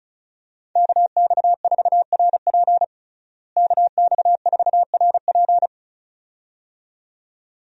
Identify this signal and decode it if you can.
Morse code, heard as a keyed tone, about 34 wpm, 700 Hz. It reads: KX4RP KX4RP
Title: USA Call Signs - 5 Characters - Repeated 2x - 35wpm